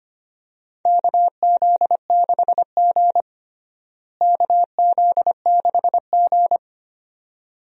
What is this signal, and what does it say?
Morse code, heard as a keyed tone, about 25 wpm, 700 Hz. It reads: KZ6G KZ6G
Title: USA Call Signs - 4 Characters - Repeated 2x - 25wpm